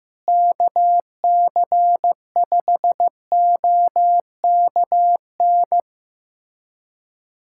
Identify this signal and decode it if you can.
Morse code, heard as a keyed tone, about 15 wpm, 700 Hz. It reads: KC5OKN
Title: USA Callsign DE Callsign 15wpm